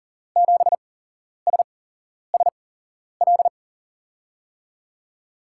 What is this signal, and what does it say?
Morse code, heard as a keyed tone, about 40 wpm, spaced as for 11 wpm, 700 Hz. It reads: 7SSL
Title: USA Call Signs - 5 Characters - 8x Character Spacing - 40wpm